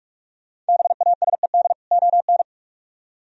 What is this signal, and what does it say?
Morse code, heard as a keyed tone, about 49 wpm, 700 Hz. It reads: BASED ON